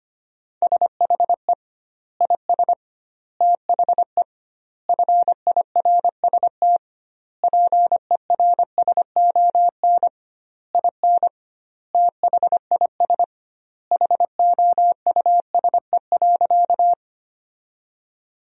Morse code, heard as a keyed tone, about 25 wpm, 700 Hz. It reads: SHE IS THE FIRST PERSON IN THIS HOUSE.
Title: 2x Word-Spacing - Sentences from Top 400 Words - 25wpm